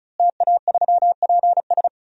Morse code, heard as a keyed tone, about 34 wpm, 700 Hz. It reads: TA3PS
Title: DXCC-100 Call Signs - 5 Characters - Repeated 3x - 35wpm